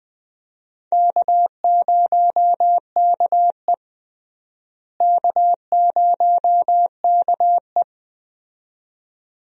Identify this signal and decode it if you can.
Morse code, heard as a keyed tone, about 20 wpm, 700 Hz. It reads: K0KE K0KE
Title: USA Call Signs - 4 Characters - Repeated 2x - 20wpm